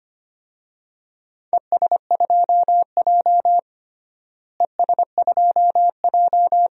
Morse code, heard as a keyed tone, about 25 wpm, 700 Hz. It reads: ES2J ES2J
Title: DXCC-100 Call Signs - 4 Characters - Repeated 2x - 25wpm